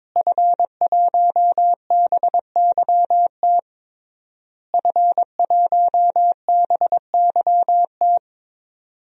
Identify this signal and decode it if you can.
Morse code, heard as a keyed tone, about 22 wpm, 700 Hz. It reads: F1BYT F1BYT
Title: DXCC-100 Call Signs - 5 Characters - Repeated 3x - 22wpm